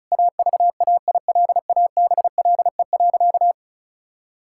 Morse code, heard as a keyed tone, about 34 wpm, 700 Hz. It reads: AVAILABLE.
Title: Sentences from Top 1000 Words - Review Part 5 - 35wpm